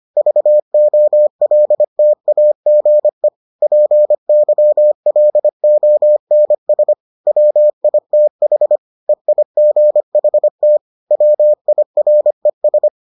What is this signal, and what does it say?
Morse code, heard as a keyed tone, about 25 wpm, 600 Hz. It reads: VOLTAGE PYLONS WITH EIGHT WIRES